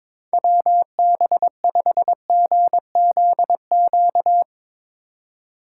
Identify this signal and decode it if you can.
Morse code, heard as a keyed tone, about 23 wpm, 700 Hz. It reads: WB5GZQ
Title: USA Call Signs - 6 Characters - 3x Character Spacing - 22wpm